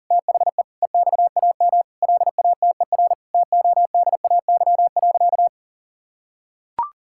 Morse code, heard as a keyed tone, about 40 wpm, 700 Hz. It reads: THE EXAM LATER TODAY.
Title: Sentences from Top 1000 Words - Review Part 5 - 40wpm